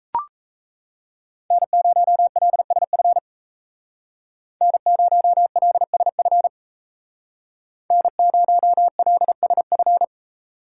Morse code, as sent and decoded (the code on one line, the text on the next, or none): -. ----- .-.. ... ..-. / -. ----- .-.. ... ..-. / -. ----- .-.. ... ..-.
N0LSF N0LSF N0LSF